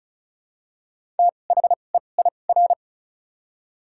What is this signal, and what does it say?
Morse code, heard as a keyed tone, about 36 wpm, spaced as for 25 wpm, 700 Hz. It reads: THEIR
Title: 2x Character Spacing - Top 100 Words - 35wpm